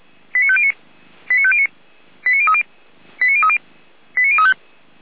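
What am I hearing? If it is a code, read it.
Five-tone selective call (ZVEI), heard as quick runs of five tones, tones of about 70 ms. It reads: 89509 89509 89030 89030 89035
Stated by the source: ZVEI five-tone selective call received with an SDR, archived by Feuerwehr Orlamuende.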